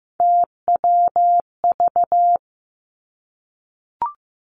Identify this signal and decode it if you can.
Morse code, heard as a keyed tone, about 15 wpm, 700 Hz. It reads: TWV
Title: Three Challenging Letter-Number Combinations 15wpm